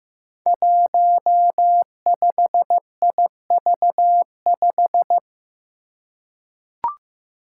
Morse code, heard as a keyed tone, about 15 wpm, 700 Hz. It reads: 15IV5
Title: Five Challenging Letter-Number Combinations 15wpm